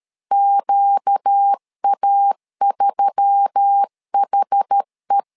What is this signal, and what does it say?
Morse code, heard as a keyed tone, about 13 wpm, 800 Hz. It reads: QA3HE
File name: LEZIONI / 5 LBVC / 5 LBVC 12wpm 1955